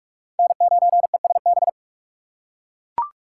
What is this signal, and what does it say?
Morse code, heard as a keyed tone, about 48 wpm, 690 Hz. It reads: N9ESB
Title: USA Call Signs - 5 Characters - 2x Character Spacing - 45wpm